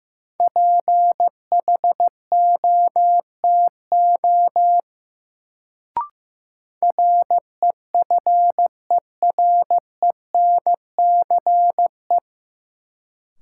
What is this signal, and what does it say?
Morse code, heard as a keyed tone, about 15 wpm, 700 Hz. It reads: PHOTO REFERENCE
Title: Sentences from Top 1200 Words - New Word Drill Alt - 15wpm